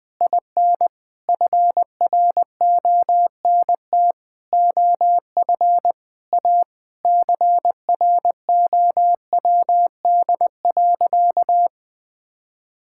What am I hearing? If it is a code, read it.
Morse code, heard as a keyed tone, about 20 wpm, 700 Hz. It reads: IN FRONT OF A CROWD.
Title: Sentences with 4-Word Phrases - Spoken Beginning - 20wpm